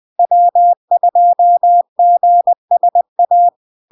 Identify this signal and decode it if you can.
Morse code, heard as a keyed tone, about 20 wpm, 700 Hz. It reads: W2GSA
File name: LARK-FD-2022-Log-in-CW-20wpm-700-Hz